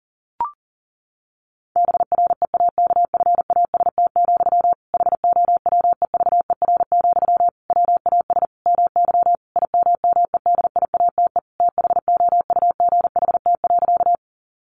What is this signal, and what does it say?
Morse code, heard as a keyed tone, about 37 wpm, 700 Hz. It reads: BREAKFAST, HOWEVER, WAS MY IMMEDIATE THOUGHT.
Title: Sentences from Top 1500 Words 40wpm